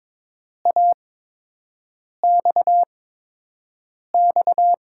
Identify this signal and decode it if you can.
Morse code, heard as a keyed tone, about 22 wpm, spaced as for 6 wpm, 700 Hz. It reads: AXX
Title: USA Call Signs - 8x Character Spacing - 22wpm